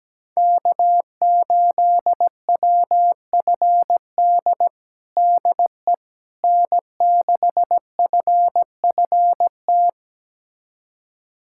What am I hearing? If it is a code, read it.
Morse code, heard as a keyed tone, about 17 wpm, 700 Hz. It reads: K8WFD DE N6FFT
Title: USA Callsign DE Callsign - 5 Characters - 17wpm